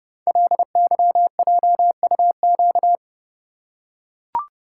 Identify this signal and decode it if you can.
Morse code, heard as a keyed tone, about 30 wpm, 700 Hz. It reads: LYJUQ
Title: Code Groups - 5 Letters - 30wpm